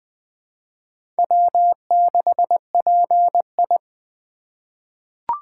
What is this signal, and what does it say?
Morse code, heard as a keyed tone, about 20 wpm, 700 Hz. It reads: W6PI
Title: USA Call Signs - 4 Characters - 4x Character Spacing - 20wpm